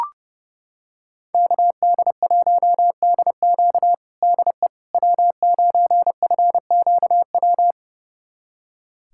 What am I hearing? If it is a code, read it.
Morse code, heard as a keyed tone, about 30 wpm, 700 Hz. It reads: KD1DQ DE W9FQW
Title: USA Callsign DE Callsign - 5 Characters - 30wpm